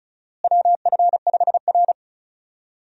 Morse code, heard as a keyed tone, about 36 wpm, 700 Hz. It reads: WF5R
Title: USA Call Signs - 4 Characters - Repeated 3x - 35wpm